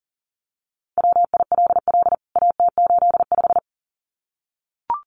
Morse code, heard as a keyed tone, about 40 wpm, 700 Hz. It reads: WILL AT85
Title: CWT Exchange 40wpm